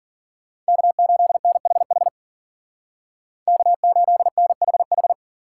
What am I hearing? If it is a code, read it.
Morse code, heard as a keyed tone, about 52 wpm, 700 Hz. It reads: K8NHH K8NHH